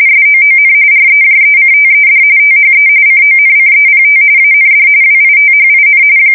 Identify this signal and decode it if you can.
RTTY, about 45.45 baud. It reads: SOON REJECTED THE CONCEPT AND WITHIN
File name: fsk170 / battleships